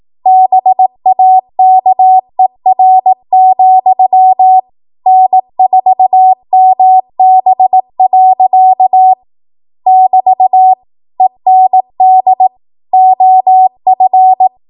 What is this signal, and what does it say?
Morse code, heard as a keyed tone, about 18 wpm, 750 Hz. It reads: BAKER, N4MB. <BT> END OF